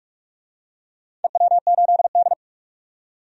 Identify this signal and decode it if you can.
Morse code, heard as a keyed tone, about 45 wpm, 700 Hz. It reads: EW8D
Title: DXCC-100 Call Signs - 4 Characters - 45wpm